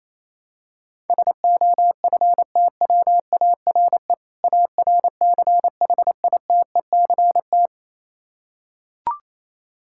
Morse code, heard as a keyed tone, about 28 wpm, 700 Hz. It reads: SOFTWARE ARCHITECT
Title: Practice QSO - Exchange 3 - More Info - 28wpm